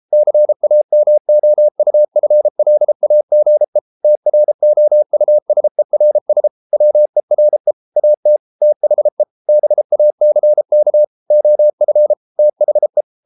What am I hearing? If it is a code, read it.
Morse code, heard as a keyed tone, about 33 wpm, 600 Hz. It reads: CAMOUFLAGE TROUSERS WERE AT THE BACK OF THE